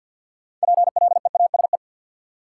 Morse code, heard as a keyed tone, about 52 wpm, 700 Hz. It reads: PLEASE
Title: Sentences from Top 1500 Words - New Word Drill Alt - 50wpm